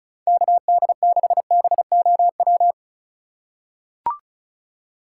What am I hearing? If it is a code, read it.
Morse code, heard as a keyed tone, about 34 wpm, 700 Hz. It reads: KD6BOW
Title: USA Call Signs - 6 Characters - 4x Character Spacing - 35wpm